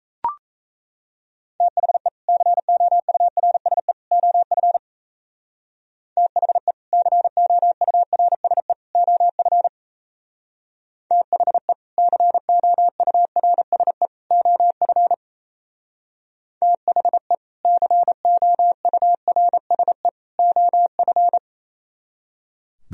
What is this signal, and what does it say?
Morse code, heard as a keyed tone, about 34 wpm, 700 Hz. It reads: THE COURSE OF THE COURSE OF THE COURSE OF THE COURSE OF
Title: Speed-Racing - Sets of 3 Words 28wpm